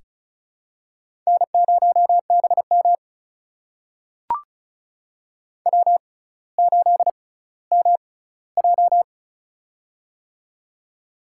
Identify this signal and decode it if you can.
Morse code, heard as a keyed tone, about 36 wpm, spaced as for 18 wpm, 700 Hz. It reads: N0BM W8MJ
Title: USA Call Signs - 6x Character Spacing - 35wpm